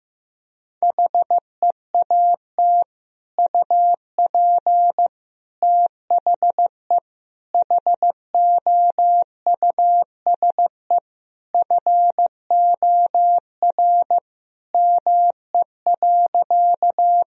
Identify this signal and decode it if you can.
Morse code, heard as a keyed tone, about 16 wpm, 700 Hz. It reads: HEAT UP THE HOUSE FOR ME.
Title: Sentences from Top 1000 Words - Review Part 3 - 15wpm